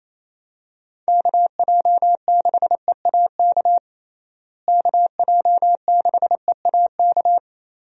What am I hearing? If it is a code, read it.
Morse code, heard as a keyed tone, about 28 wpm, 700 Hz. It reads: KJ6EAK KJ6EAK
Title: USA Call Signs - 6 Characters - Repeated 2x - 28wpm